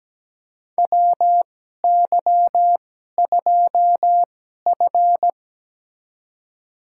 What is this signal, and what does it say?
Morse code, heard as a keyed tone, about 17 wpm, spaced as for 12 wpm, 700 Hz. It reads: WY2F
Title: USA Call Signs - 4 Characters - 2x Character Spacing - 17wpm